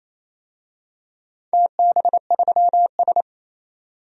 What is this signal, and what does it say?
Morse code, heard as a keyed tone, about 28 wpm, 700 Hz. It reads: TB3S
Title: DXCC-100 Call Signs - 4 Characters - 28wpm